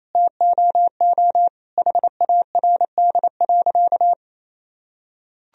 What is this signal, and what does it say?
Morse code, heard as a keyed tone, about 28 wpm, 700 Hz. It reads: TOO HARD.
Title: Sentences from Top 1000 Words - Review Part 1 - 28wpm